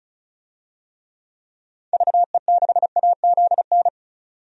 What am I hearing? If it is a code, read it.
Morse code, heard as a keyed tone, about 34 wpm, 700 Hz. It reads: VE6AZN
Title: DXCC-100 Call Signs - 6 Characters - 35wpm